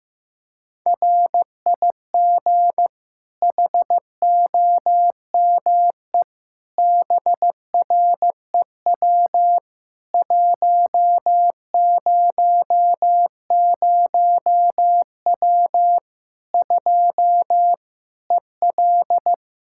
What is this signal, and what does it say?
Morse code, heard as a keyed tone, about 15 wpm, 700 Hz. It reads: RIG HOME BREW 100W 2 EL
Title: Practice QSO - Exchange 2 - Info - 15wpm